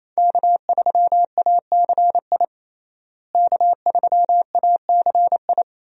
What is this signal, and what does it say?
Morse code, heard as a keyed tone, about 27 wpm, 700 Hz. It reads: K3ACI K3ACI
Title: USA Call Signs - 5 Characters - Repeated 2x - 28wpm